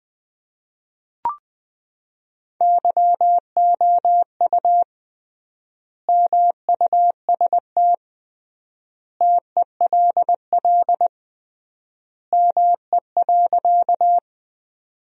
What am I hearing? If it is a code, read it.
Morse code, heard as a keyed tone, about 21 wpm, 700 Hz. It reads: YOU MUST TELL ME.
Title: 3x Word-Spacing - Sentences from Top 200 Words - 20wpm